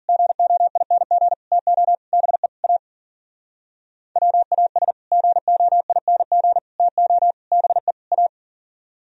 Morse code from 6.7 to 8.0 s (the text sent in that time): TO BE